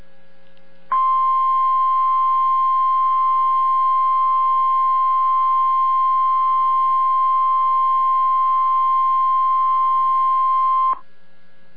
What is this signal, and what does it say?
Emergency Alert System (SAME) tones (the 1050 Hz attention tone). It is heard from 0.9 to 11.0 s.